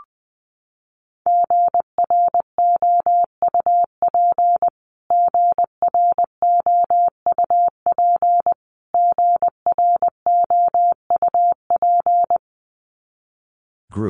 Morse code, heard as a keyed tone, about 20 wpm, 700 Hz. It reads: GROUP GROUP GROUP
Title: Top 500 Words Repeated 20wpm